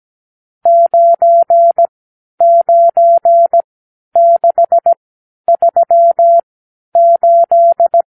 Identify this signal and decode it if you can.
Morse code, heard as a keyed tone, about 17 wpm, spaced as for 10 wpm, 680 Hz. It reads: 99638
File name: arrl / Review (1-0)